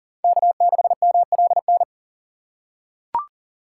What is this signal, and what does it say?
Morse code, heard as a keyed tone, about 40 wpm, 700 Hz. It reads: K6MLN
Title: USA Call Signs - 5 Characters - 2x Character Spacing - 40wpm